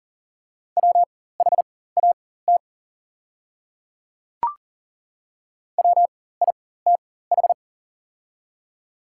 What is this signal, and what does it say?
Morse code, heard as a keyed tone, about 40 wpm, spaced as for 19 wpm, 700 Hz. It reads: WHAT WITH
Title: 4x Character Spacing - Top 100 Words - 40wpm